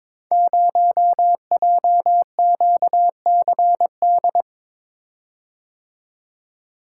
Morse code, heard as a keyed tone, about 22 wpm, 700 Hz. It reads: 0JQCD
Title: Code Groups - 5 Letter:Numbers - 22wpm